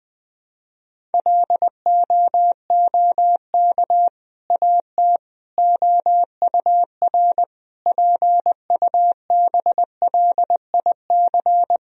Morse code, heard as a keyed tone, about 20 wpm, 700 Hz. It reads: LOOK AT OUR PUBLIC
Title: Sentences from Top 1000 Words - Review Part 3 - 20wpm